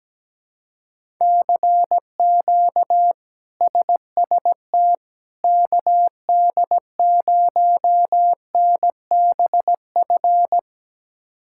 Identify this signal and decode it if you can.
Morse code, heard as a keyed tone, about 17 wpm, 700 Hz. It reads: CQ SST KD0NBF
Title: K1USN SST QSO Practice - 17wpm